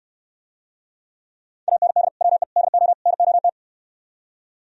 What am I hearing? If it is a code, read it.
Morse code, heard as a keyed tone, about 68 wpm, 700 Hz. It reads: AND WE DO NOT